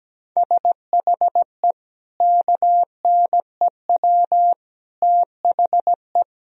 Morse code, heard as a keyed tone, about 17 wpm, 700 Hz. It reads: SHE KNEW THE
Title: Sentences from Top 1000 Words - Review Part 4 - 17wpm